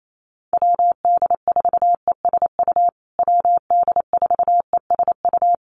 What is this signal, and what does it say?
Morse code, heard as a keyed tone, about 28 wpm, 690 Hz. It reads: WD4ESU WD4ESU
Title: USA Call Signs Repeated 3x 28wpm